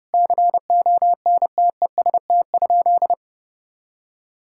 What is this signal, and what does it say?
Morse code, heard as a keyed tone, about 30 wpm, 700 Hz. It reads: CONTEST?
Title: Sentences from Top 1600 Words 30wpm